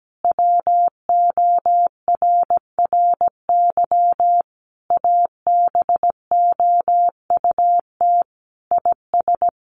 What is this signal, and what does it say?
Morse code, heard as a keyed tone, about 17 wpm, 700 Hz. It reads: WORRY ABOUT IS
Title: A Sound Of Thunder 17wpm